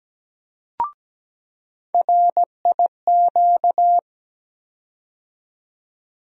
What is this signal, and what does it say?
Morse code, heard as a keyed tone, about 17 wpm, 700 Hz. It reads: RIQ
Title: USA Call Sign Postfixes - 3 Letters 17wpm